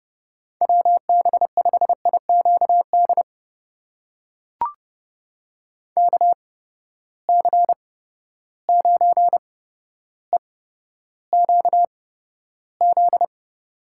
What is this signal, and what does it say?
Morse code, heard as a keyed tone, about 30 wpm, spaced as for 13 wpm, 700 Hz. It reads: WB5IQD KC9EQZ
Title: USA Call Signs - 6 Characters - 8x Character Spacing - 30wpm